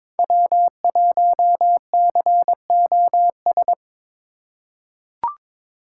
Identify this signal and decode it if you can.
Morse code, heard as a keyed tone, about 22 wpm, 700 Hz. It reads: W1COS
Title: USA Call Signs - 5 Characters - 3x Character Spacing - 22wpm